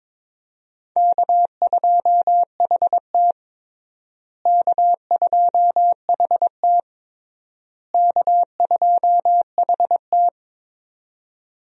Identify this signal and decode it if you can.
Morse code, heard as a keyed tone, about 21 wpm, 700 Hz. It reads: K2HT K2HT K2HT
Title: USA Call Signs - 4 Characters - Repeated 3x - 22wpm